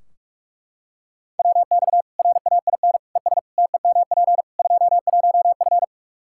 Morse code, heard as a keyed tone, about 41 wpm, 700 Hz. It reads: WX RAIN ES TEMP 21F
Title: Practice QSO - Exchange 2 - Info - 45wpm